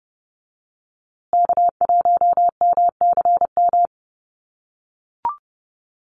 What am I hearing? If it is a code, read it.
Morse code, heard as a keyed tone, about 30 wpm, 700 Hz. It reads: K1MCM
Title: USA Call Signs Repeated 3x 30wpm